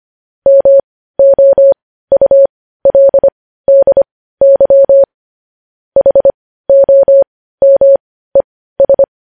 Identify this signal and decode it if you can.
Morse code, heard as a keyed tone, about 25 wpm, spaced as for 15 wpm, 550 Hz. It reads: MOULDY HOMES
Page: News Headlines in Morse Code at 15 WPM